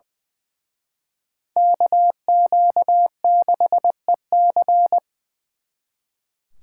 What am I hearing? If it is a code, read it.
Morse code, heard as a keyed tone, about 23 wpm, 700 Hz. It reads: KQ6EC
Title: Speed Racing - USA 2x2 Call Signs 20wpm